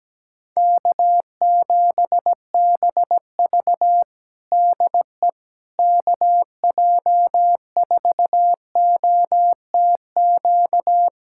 Morse code, heard as a keyed tone, about 17 wpm, 700 Hz. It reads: K7BV DE KJ4OTQ
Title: Practice QSO - Exchange 4 - Ending - 17wpm